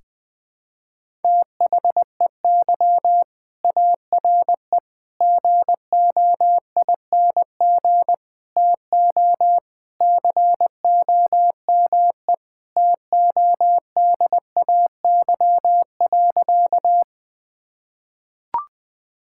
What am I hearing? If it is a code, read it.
Morse code, heard as a keyed tone, about 21 wpm, 700 Hz. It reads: THEY ARE GOING TO COME TODAY.
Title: Sentences with 3-Word Phrases - Spoken Beginning - 20wpm